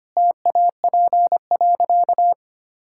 Morse code, heard as a keyed tone, about 25 wpm, 700 Hz. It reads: TAP.
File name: Easy Sentences with Hints - 25wpm